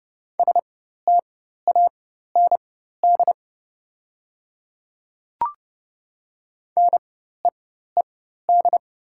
Morse code, heard as a keyed tone, about 30 wpm, spaced as for 14 wpm, 700 Hz. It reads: STAND NEED
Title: 4x Character Spacing - Top 300 Words - 30wpm